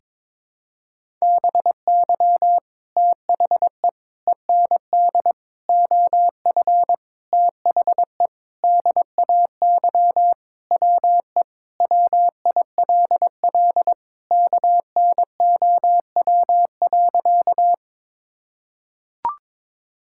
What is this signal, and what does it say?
Morse code, heard as a keyed tone, about 22 wpm, 700 Hz. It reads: BY THE END OF THE DAY WE WILL KNOW.
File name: practice / 22wpm / Sentences with 4-Word Phrases - 22wpm